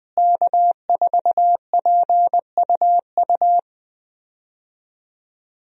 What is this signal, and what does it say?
Morse code, heard as a keyed tone, about 20 wpm, 700 Hz. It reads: K4PUU
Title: USA Call Signs - 5 Characters - Repeated 3x - 20wpm